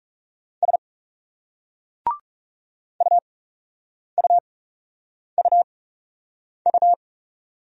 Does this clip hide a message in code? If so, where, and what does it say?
Yes: Morse code from 0.6 to 6.9 s, heard as a keyed tone, about 41 wpm, spaced as for 8 wpm, 700 Hz. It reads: S UUUU